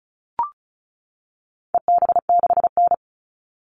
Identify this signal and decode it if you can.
Morse code, heard as a keyed tone, about 34 wpm, 700 Hz. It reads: EB6N
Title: All Cut Numbers - 4-Digit 35wpm